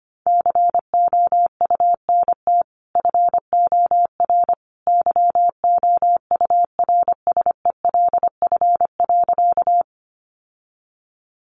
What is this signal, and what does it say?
Morse code, heard as a keyed tone, about 25 wpm, 700 Hz. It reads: COUNT FOR YOURSELF.
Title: Sentences from Top 1500 Words 25wpm